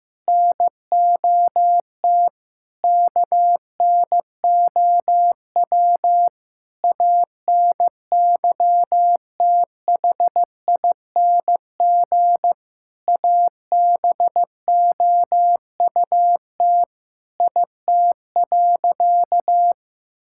Morse code, heard as a keyed tone, about 15 wpm, 700 Hz. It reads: NOT KNOW ANYTHING ABOUT IT.
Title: Sentences from Top 1600 Words 15wpm